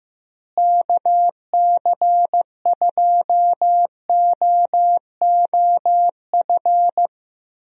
Morse code, heard as a keyed tone, about 15 wpm, 700 Hz. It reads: KC2OOF
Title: USA Call Signs - 6 Characters - Repeated 3x - 15wpm